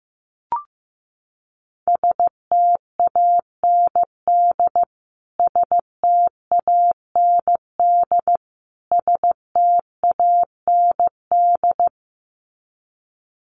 Morse code, heard as a keyed tone, about 15 wpm, 700 Hz. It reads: STAND STAND STAND
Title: Top 500 Words Repeated 15wpm